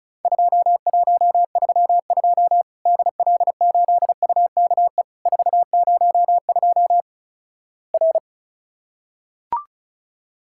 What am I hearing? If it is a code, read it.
Morse code, heard as a keyed tone, about 36 wpm, 700 Hz. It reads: 2132 DL8UKE 402 R
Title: Worked All Europe DX Contest - QTC Exchange - 35wpm